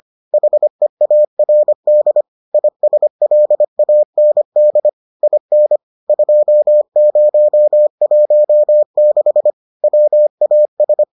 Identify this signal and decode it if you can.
Morse code, heard as a keyed tone, about 25 wpm, 600 Hz. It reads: HEARD ISLAND IN 2016 WAS